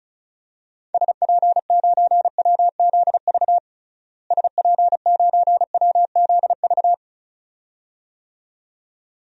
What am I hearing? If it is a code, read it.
Morse code, heard as a keyed tone, about 36 wpm, 700 Hz. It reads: SP9WZV SP9WZV
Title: DXCC-100 Call Signs - 6 Characters - Repeated 2x - 35wpm